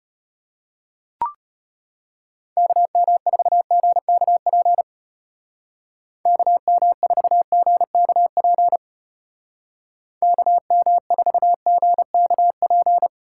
Morse code, heard as a keyed tone, about 32 wpm, 700 Hz. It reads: KM4GKP KM4GKP KM4GKP